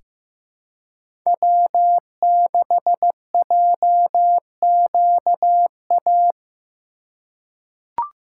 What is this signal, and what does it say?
Morse code, heard as a keyed tone, about 15 wpm, 700 Hz. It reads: W6JQA
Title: USA Call Signs - 5 Characters - 2x Character Spacing - 15wpm